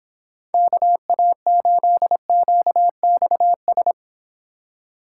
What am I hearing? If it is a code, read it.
Morse code, heard as a keyed tone, about 24 wpm, 700 Hz. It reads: KA8QXS